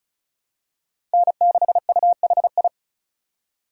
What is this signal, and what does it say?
Morse code, heard as a keyed tone, about 36 wpm, 700 Hz. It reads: N6UHI
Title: USA Call Signs - 5 Characters - 3x Character Spacing - 35wpm